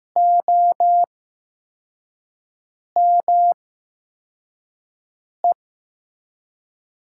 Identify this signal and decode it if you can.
Morse code, heard as a keyed tone, about 15 wpm, spaced as for 4 wpm, 700 Hz. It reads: OME
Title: USA Call Signs - 6 Characters - 8x Character Spacing - 15wpm